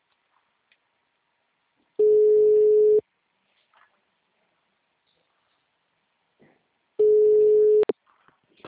Ringback tone (Europe, 425 Hz).
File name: out- -2014- - - .2